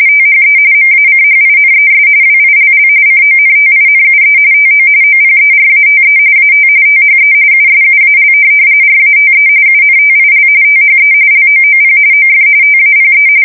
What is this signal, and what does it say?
RTTY, about 45.45 baud. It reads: THE MACHINES WILL CONTINUE TO OPERATE AT THE USUAL RATE UNTIL THE LOOP OF TAPE
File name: fsk170 / printing-telegraph